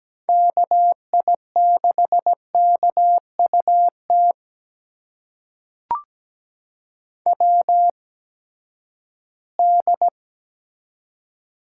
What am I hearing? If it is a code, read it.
Morse code, heard as a keyed tone, about 17 wpm, spaced as for 12 wpm, 700 Hz. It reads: KI6KUT WD
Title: USA Call Signs - 8x Character Spacing - 17wpm